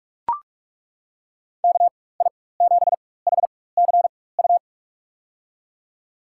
Morse code, heard as a keyed tone, about 42 wpm, spaced as for 20 wpm, 700 Hz. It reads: KI7HCU
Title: USA Call Signs - 6 Characters - 4x Character Spacing - 45wpm